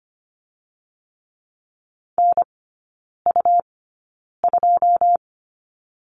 Morse code, heard as a keyed tone, about 25 wpm, spaced as for 9 wpm, 700 Hz. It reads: NU2
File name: Lesson 17e - Letter u - Three Characters 25wpm